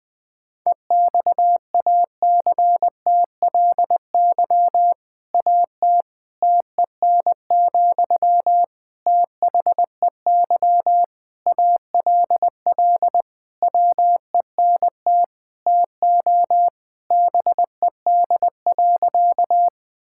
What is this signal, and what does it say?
Morse code, heard as a keyed tone, about 20 wpm, 700 Hz. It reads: EXACTLY AT TEN, THEY ALL WENT TO BED.
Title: Sentences from Top 1000 Words - Review Part 6 - 20wpm